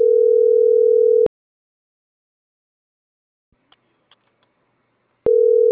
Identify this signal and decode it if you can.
Ringback tone (North America), 440 + 480 Hz.